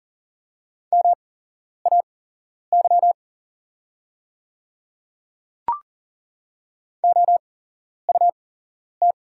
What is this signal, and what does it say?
Morse code, heard as a keyed tone, about 41 wpm, spaced as for 11 wpm, 700 Hz. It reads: MAY OUT
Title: 8x Character Spacing - Top 100 Words - 40wpm